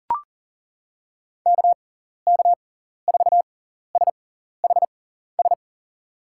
Morse code, heard as a keyed tone, about 40 wpm, spaced as for 14 wpm, 700 Hz. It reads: KK4SHS
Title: USA Call Signs - 6 Characters - 6x Character Spacing - 40wpm